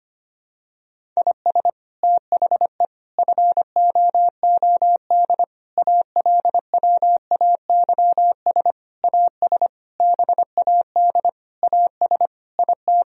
Morse code, heard as a keyed tone, about 25 wpm, 700 Hz. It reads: IS THE FOOD ALWAYS AS BAD AS IT
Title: Sentences from Top 1000 Words - Review Part 1 - 25wpm